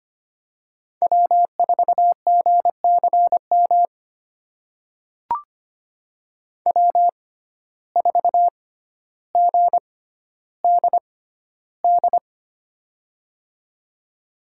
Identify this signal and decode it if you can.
Morse code, heard as a keyed tone, about 25 wpm, spaced as for 13 wpm, 700 Hz. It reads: W4GCM W4GDD
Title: USA Call Signs - 5 Characters - 6x Character Spacing - 25wpm